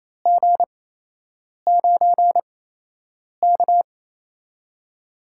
Morse code, heard as a keyed tone, about 28 wpm, spaced as for 8 wpm, 700 Hz. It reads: G9K
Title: USA Call Signs - 4 Characters - 8x Character Spacing - 28wpm